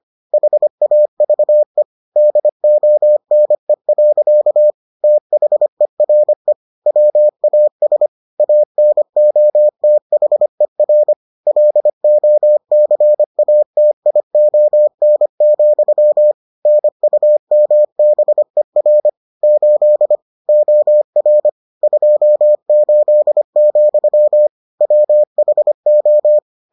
Morse code, heard as a keyed tone, about 25 wpm, 600 Hz. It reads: HAVE DONE. THERE WAS ANOTHER LOCATION, NUMBER 8 OR 28, WHO